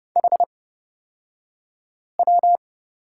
Morse code, heard as a keyed tone, about 30 wpm, spaced as for 5 wpm, 700 Hz. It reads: HW
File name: course / 30:10wpm / 235 Lesson 38d - Letter q - Two Characters 30wpm